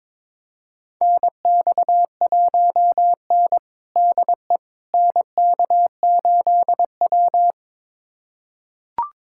Morse code, heard as a keyed tone, about 22 wpm, 700 Hz. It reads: NX1N DE NK8W
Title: USA Callsign DE Callsign - 4 Characters - 22wpm